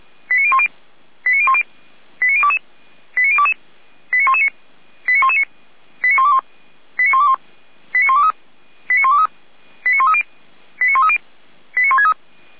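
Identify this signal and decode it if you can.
Five-tone selective call (ZVEI), heard as quick runs of five tones, tones of about 70 ms. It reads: 89010 89010 89022 89022 89119 89119 89121 89121 89123 89123 89130 89130 89163
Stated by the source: ZVEI five-tone selective call received via SDR.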